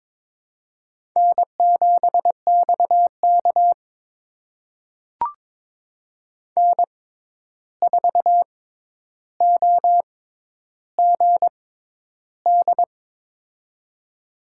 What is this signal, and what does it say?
Morse code, heard as a keyed tone, about 21 wpm, spaced as for 10 wpm, 700 Hz. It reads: N7XK N4OGD
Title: USA Call Signs - 6x Character Spacing - 22wpm